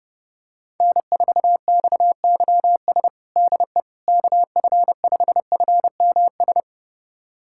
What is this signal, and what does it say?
Morse code, heard as a keyed tone, about 30 wpm, 700 Hz. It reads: N4XYS DE KF5FMS
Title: USA Callsign DE Callsign 30wpm